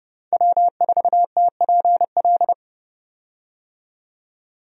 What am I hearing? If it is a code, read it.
Morse code, heard as a keyed tone, about 30 wpm, 700 Hz. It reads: W4TPL
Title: USA Call Signs - 5 Characters - 30wpm